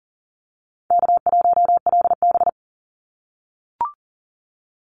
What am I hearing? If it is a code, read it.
Morse code, heard as a keyed tone, about 40 wpm, 700 Hz. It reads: K1LB
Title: NA SOTA Activator Callsigns 40wpm